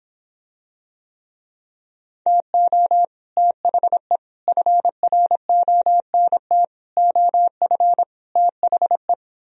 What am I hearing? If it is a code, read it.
Morse code, heard as a keyed tone, about 26 wpm, 700 Hz. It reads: TO THE FRONT OF THE